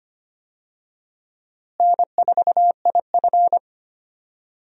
Morse code, heard as a keyed tone, about 25 wpm, 700 Hz. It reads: N4IF
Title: USA Call Signs - 4 Characters - Repeated 2x - 25wpm